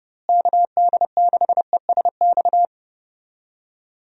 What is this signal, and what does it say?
Morse code, heard as a keyed tone, about 31 wpm, 700 Hz. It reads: KD6ESX
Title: USA Call Signs - 6 Characters - Repeated 2x - 30wpm